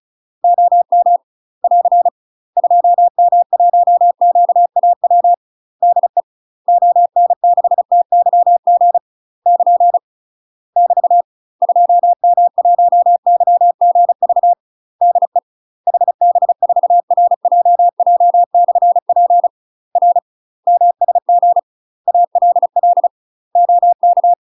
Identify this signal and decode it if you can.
Morse code, heard as a keyed tone, about 36 wpm, 700 Hz. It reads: OM <AR> 2M1QAW DE ON6TYG <KN> <BT> 2M1YGV DE HB4RJJ/P R MSG ALL OK
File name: UK MASTER QSOs TRAINING / 35WPM / UK MASTER QSOs 0028